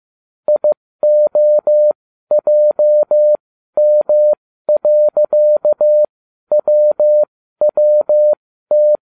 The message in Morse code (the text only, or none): IOJM.WWT